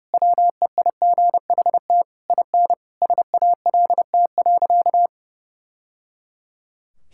Morse code, heard as a keyed tone, about 30 wpm, 700 Hz. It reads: WEIGHT IN SALT.
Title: Sentences from Top 1000 Words - Review Part 5 - 30wpm